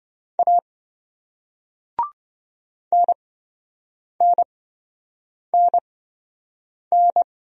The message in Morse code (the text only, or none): A NNNN